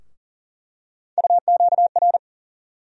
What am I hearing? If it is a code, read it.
Morse code, heard as a keyed tone, about 40 wpm, 700 Hz. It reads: UQR
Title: USA Call Sign Postfixes - 3 Letters 40wpm